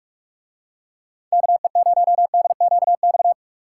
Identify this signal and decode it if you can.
Morse code, heard as a keyed tone, about 44 wpm, 700 Hz. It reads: KE0DQX